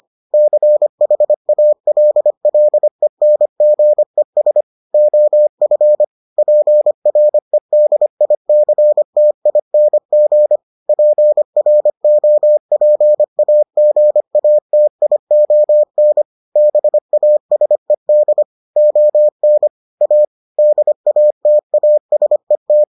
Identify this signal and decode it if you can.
Morse code, heard as a keyed tone, about 25 wpm, 600 Hz. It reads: CHALLENGES OF PREDICTING PROPAGATION BASED ON A DATASET